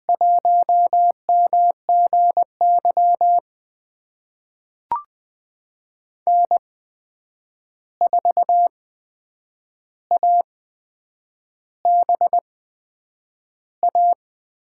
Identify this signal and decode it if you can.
Morse code, heard as a keyed tone, about 20 wpm, spaced as for 8 wpm, 700 Hz. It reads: 1MGY N4ABA
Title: USA Call Signs - 5 Characters - 8x Character Spacing - 20wpm